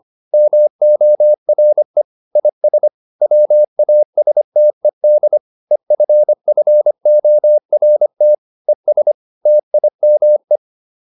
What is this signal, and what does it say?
Morse code, heard as a keyed tone, about 25 wpm, 600 Hz. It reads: MORE IS WASTED EFFORT ES TIME